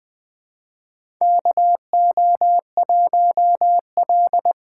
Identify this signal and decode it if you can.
Morse code, heard as a keyed tone, about 20 wpm, 700 Hz. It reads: KO1L
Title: USA Call Signs - 4 Characters - 3x Character Spacing - 20wpm